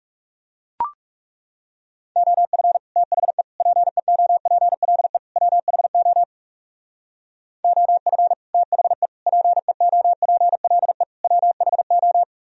Morse code, heard as a keyed tone, about 36 wpm, 700 Hz. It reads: OF THE PEOPLE WHO OF THE PEOPLE WHO
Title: Speed-Racing - Sets of 4 Words 30wpm